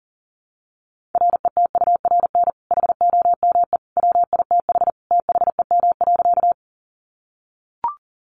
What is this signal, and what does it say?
Morse code, heard as a keyed tone, about 40 wpm, 700 Hz. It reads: RETURN HOME WITH THEM.
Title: Sentences from Top 800 Words 40wpm